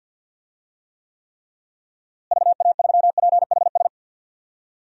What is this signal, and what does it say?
Morse code, heard as a keyed tone, about 52 wpm, 700 Hz. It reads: VA3PHS
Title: DXCC-100 Call Signs - 6 Characters - Repeated 3x - 50wpm